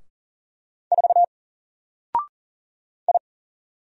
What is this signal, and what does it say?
Morse code, heard as a keyed tone, about 39 wpm, 700 Hz. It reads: 4 I
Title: Single Character (Letter:Number:Punctuation:Prosign) 40wpm